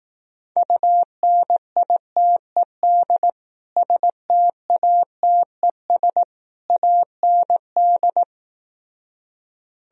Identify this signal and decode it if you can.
Morse code, heard as a keyed tone, about 18 wpm, 700 Hz. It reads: UNITED STATES AND
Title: Speed-Racing - Sets of 3 Words 15wpm